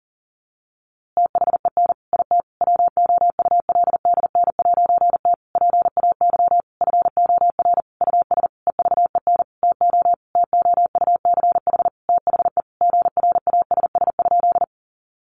Morse code, heard as a keyed tone, about 40 wpm, 700 Hz. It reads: THEN IT WOULDN'T PAY FOR US EVEN TO TOUCH THE GRASS?
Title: A Sound Of Thunder 40wpm